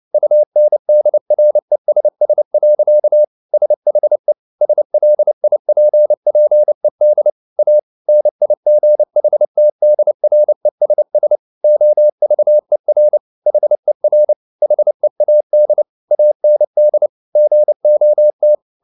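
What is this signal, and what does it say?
Morse code, heard as a keyed tone, about 29 wpm, 600 Hz. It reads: UNDRESS. SHE SLIPPED A NIGHTDRESS OVER HER HEAD AND GOT
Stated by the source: Morse code at 29 wpm, sending Ken Follett's novel Never in English, from CW med Gnister.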